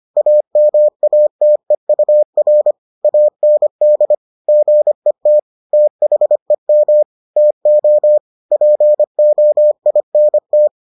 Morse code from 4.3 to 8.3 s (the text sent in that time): GET THEM TO